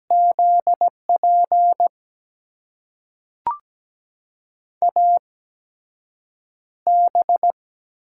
Morse code, heard as a keyed tone, about 17 wpm, spaced as for 5 wpm, 700 Hz. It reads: ZPAB
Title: USA Call Signs - 5 Characters - 8x Character Spacing - 17wpm